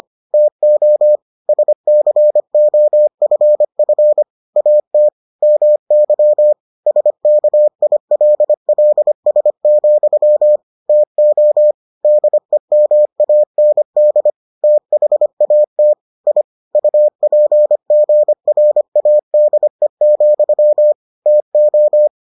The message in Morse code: TO SCOFF AT MY SKILLS, TO DEMAND THAT I UPGRADE, TO

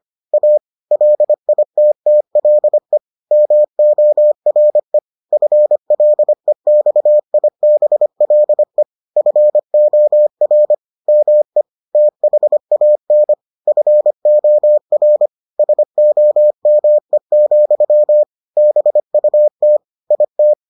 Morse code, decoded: A LITTLE MORE FLEXIBLE FOR ME THAN FOR SOME, BUT IT

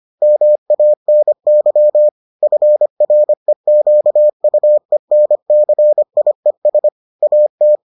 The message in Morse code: MANY FREQUENCIES AT

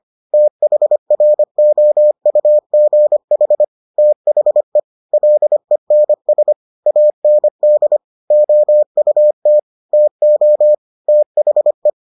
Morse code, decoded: THROUGH THE LENS AND OUT TO THE

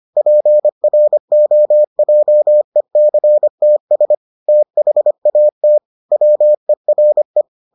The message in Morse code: PROJECTS THAT WERE

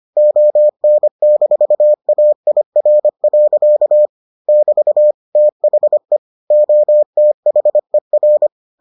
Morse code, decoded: ON-AIR. <BT> THE OTHER

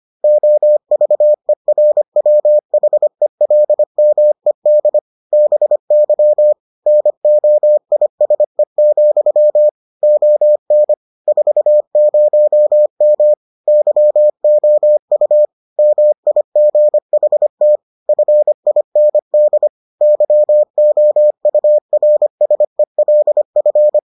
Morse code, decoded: OVERWHELMED BY NOISE, ON 40M YOU MIGHT FIND YOURSELF

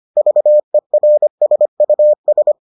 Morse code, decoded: VERSUS

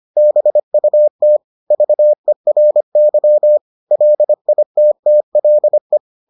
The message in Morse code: BUT VERY LITTLE

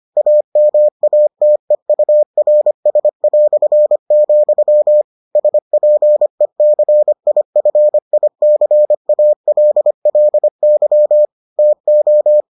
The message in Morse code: AMATEURS", SPECIFICALLY TO

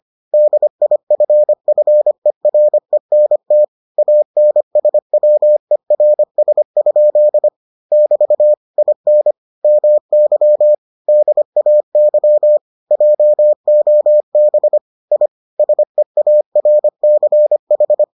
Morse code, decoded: DIFFERENT ANSWERS? <BT> IN MY DAY JOB I SEARCH

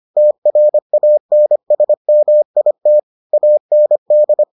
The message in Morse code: TRANSMIT AND

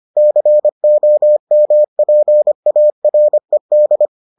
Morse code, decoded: COMPARED

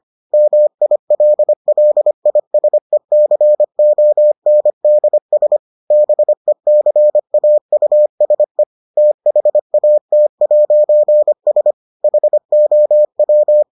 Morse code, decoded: MILLISECONDS BECAUSE THAT'S HOW

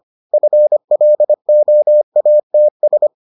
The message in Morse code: FLOATS

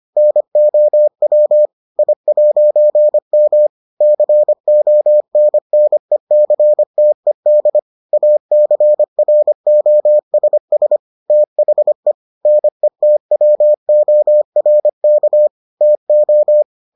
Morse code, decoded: NOW I'M CONNECTED ACROSS THE NETWORK TO

